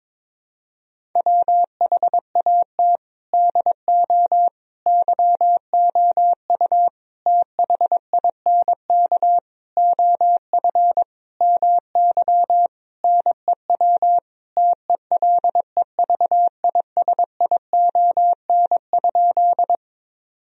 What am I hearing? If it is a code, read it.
Morse code, heard as a keyed tone, about 22 wpm, 700 Hz. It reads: WHAT DO YOU THINK OF MY NEW TELEVISION?
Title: Sentences from Top 1000 Words - Review Part 4 - 22wpm